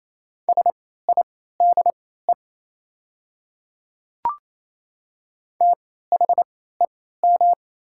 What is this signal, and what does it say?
Morse code, heard as a keyed tone, about 28 wpm, spaced as for 16 wpm, 700 Hz. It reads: SIDE THEM
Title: 3x Character Spacing - Top 300 Words - 28wpm